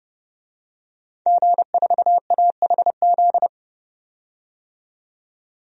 Morse code, heard as a keyed tone, about 31 wpm, 700 Hz. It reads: G4AHZ
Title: DXCC-100 Call Signs - 5 Characters - 30wpm